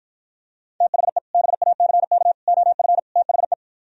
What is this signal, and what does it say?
Morse code, heard as a keyed tone, about 52 wpm, 700 Hz. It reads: THE BACK OF THE